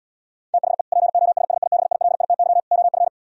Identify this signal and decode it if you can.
Morse code, heard as a keyed tone, about 77 wpm, 700 Hz. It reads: THE POSSIBILITY OF